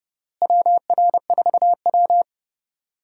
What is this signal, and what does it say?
Morse code, heard as a keyed tone, about 30 wpm, 700 Hz. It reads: WR4W
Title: USA Call Signs - 4 Characters - Repeated 3x - 30wpm